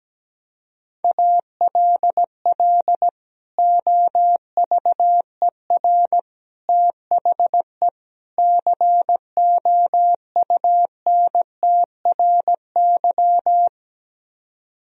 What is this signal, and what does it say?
Morse code, heard as a keyed tone, about 17 wpm, 700 Hz. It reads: ALL OVER THE COUNTRY
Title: Speed-Racing - Sets of 4 Words 17wpm